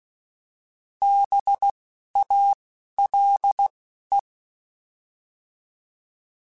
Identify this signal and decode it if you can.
Morse code, heard as a keyed tone, about 16 wpm, 800 Hz. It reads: B A L E